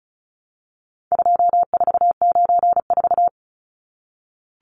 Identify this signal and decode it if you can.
Morse code, heard as a keyed tone, about 34 wpm, 700 Hz. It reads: 2494
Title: Numbers - 4-Digit 35wpm